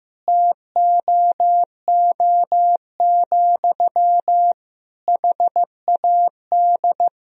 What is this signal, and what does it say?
Morse code, heard as a keyed tone, about 15 wpm, 700 Hz. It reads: TOO, HAD
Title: Sentences from Top 1000 Words - Review Part 3 - 15wpm